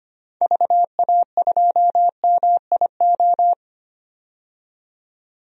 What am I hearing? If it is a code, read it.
Morse code, heard as a keyed tone, about 25 wpm, 700 Hz. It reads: VA2MIO
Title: DXCC-100 Call Signs - 6 Characters - 25wpm